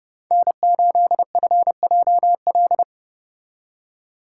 Morse code, heard as a keyed tone, about 34 wpm, 700 Hz. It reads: N8FJL